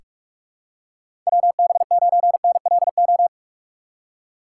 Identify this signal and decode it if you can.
Morse code, heard as a keyed tone, about 46 wpm, 700 Hz. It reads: WB9NLO
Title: USA Call Signs - 4x Character Spacing - 45wpm